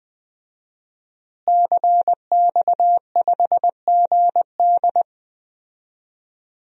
Morse code, heard as a keyed tone, about 20 wpm, 700 Hz. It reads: CX5GD
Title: DXCC-100 Call Signs - 5 Characters - Repeated 2x - 20wpm